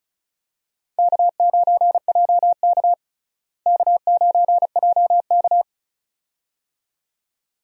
Morse code, heard as a keyed tone, about 36 wpm, 700 Hz. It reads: K9JK K9JK
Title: USA Call Signs - 4 Characters - Repeated 2x - 35wpm